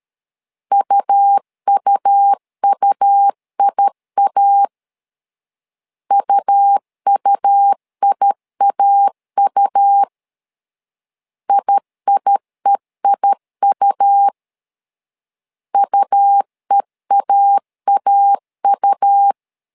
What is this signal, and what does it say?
Morse code, heard as a keyed tone, about 13 wpm, 800 Hz. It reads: UUUIA UUIAU IIEIU UEAAU
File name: LEZIONI / 2 UHDN / 2 U 12wpm 1338